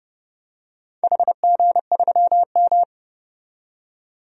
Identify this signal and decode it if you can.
Morse code, heard as a keyed tone, about 30 wpm, 700 Hz. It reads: HG3M
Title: DXCC-100 Call Signs - 4 Characters - 30wpm